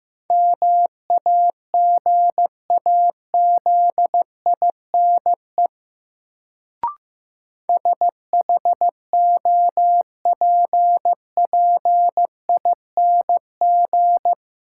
Morse code, heard as a keyed tone, about 15 wpm, 700 Hz. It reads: MAGAZINE SHOPPING
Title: Sentences from Top 1000 Words - Review Part 7 - New Word Drill - 15wpm